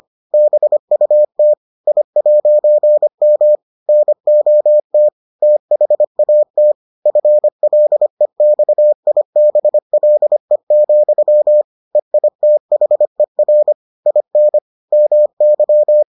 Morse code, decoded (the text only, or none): BUT I'M NOT THAT FLEXIBLE, EITHER IN MY